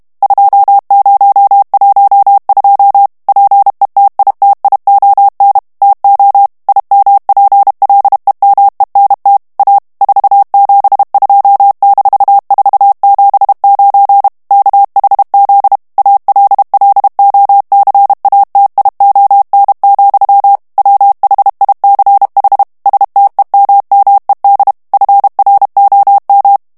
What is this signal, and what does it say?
Morse code, heard as a keyed tone, about 32 wpm, 800 Hz. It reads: 2012 PETITION TO IMPLEMENT A 472-479 KHZ ALLOCATION, WHICH STEMMED FROM
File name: morse mp3 / ANGLAIS texte clair 3 vitesse 30 mots minute